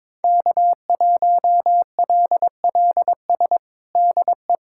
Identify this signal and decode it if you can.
Morse code, heard as a keyed tone, about 22 wpm, 700 Hz. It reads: K1LLS DE
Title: USA Callsign DE Callsign - 5 Characters - 22wpm